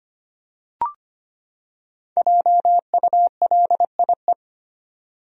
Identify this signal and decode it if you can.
Morse code, heard as a keyed tone, about 25 wpm, 700 Hz. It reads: JULIE
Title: USA Common Names - Max 5 Letters - 25wpm